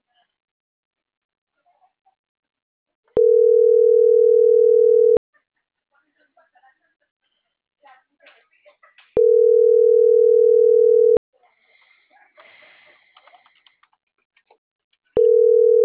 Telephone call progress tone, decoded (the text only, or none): ringback tone